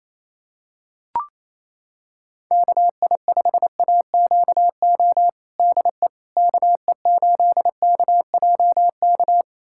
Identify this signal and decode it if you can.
Morse code, heard as a keyed tone, about 28 wpm, 700 Hz. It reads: KI5AQO DE KE8KJK
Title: USA Callsign DE Callsign 28wpm